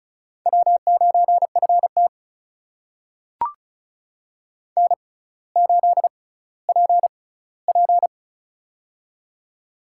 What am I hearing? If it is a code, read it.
Morse code, heard as a keyed tone, about 36 wpm, spaced as for 18 wpm, 700 Hz. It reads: W9FT N8PP
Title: USA Call Signs - 4 Characters - 6x Character Spacing - 35wpm